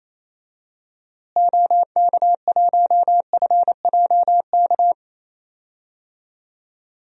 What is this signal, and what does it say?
Morse code, heard as a keyed tone, about 28 wpm, 700 Hz. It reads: OK1FJK
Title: DXCC-100 Call Signs - 6 Characters - Repeated 3x - 28wpm